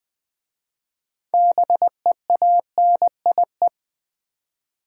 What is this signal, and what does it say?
Morse code, heard as a keyed tone, about 20 wpm, 700 Hz. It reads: BEANIE
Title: Easy Words with Hints - 20wpm